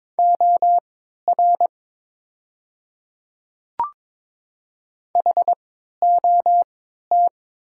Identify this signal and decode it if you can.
Morse code, heard as a keyed tone, about 22 wpm, spaced as for 13 wpm, 700 Hz. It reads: OR HOT
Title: 3x Character Spacing - Top 100 Words - 22wpm